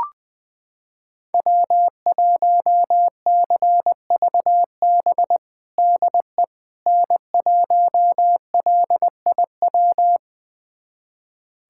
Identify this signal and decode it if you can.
Morse code, heard as a keyed tone, about 20 wpm, 700 Hz. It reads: W1CVB DE N1LIW
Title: USA Callsign DE Callsign - 5 Characters - 20wpm